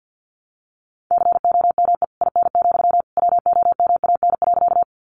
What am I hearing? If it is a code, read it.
Morse code, heard as a keyed tone, about 50 wpm, 700 Hz. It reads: COME IN, WOMAN.
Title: Sentences from Top 1100 Words 50wpm